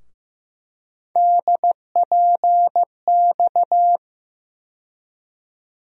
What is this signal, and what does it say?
Morse code, heard as a keyed tone, about 15 wpm, 700 Hz. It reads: DPX
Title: USA Call Sign Postfixes - 3 Letters 15wpm